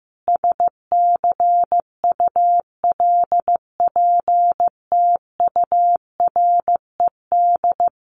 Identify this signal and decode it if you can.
Morse code, heard as a keyed tone, about 15 wpm, 700 Hz. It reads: SCULPTURED
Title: A Sound Of Thunder 15wpm